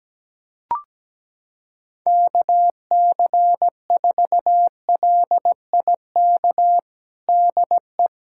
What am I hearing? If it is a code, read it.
Morse code, heard as a keyed tone, about 17 wpm, 700 Hz. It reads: KC4LIK DE
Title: USA Callsign DE Callsign - 6 Characters - 17wpm